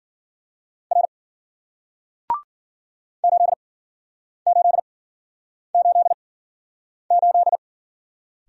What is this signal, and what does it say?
Morse code, heard as a keyed tone, about 49 wpm, spaced as for 9 wpm, 700 Hz. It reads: R 8888
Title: Speed-Racing Single Letter-Number 40wpm